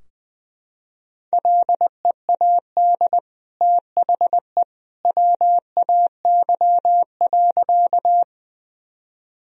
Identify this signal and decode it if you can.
Morse code, heard as a keyed tone, about 21 wpm, 700 Hz. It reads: LEAD THE WAY.
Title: Sentences from Top 1000 Words - Review Part 2 - 20wpm